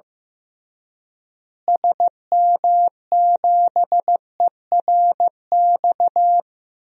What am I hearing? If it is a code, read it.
Morse code, heard as a keyed tone, about 16 wpm, 700 Hz. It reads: SM7ERX
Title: DXCC-100 Call Signs - 6 Characters - Repeated 3x - 15wpm